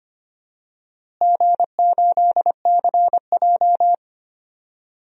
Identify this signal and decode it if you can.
Morse code, heard as a keyed tone, about 25 wpm, 700 Hz. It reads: G8CJ
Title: DXCC-100 Call Signs - 4 Characters - Repeated 3x - 25wpm